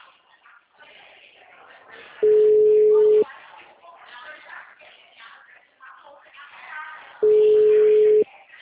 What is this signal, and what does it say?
Ringback tone (Europe, 425 Hz), with speech.